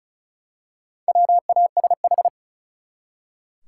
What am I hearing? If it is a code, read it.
Morse code, heard as a keyed tone, about 34 wpm, 700 Hz. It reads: WASH
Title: Four Letter Words - 35wpm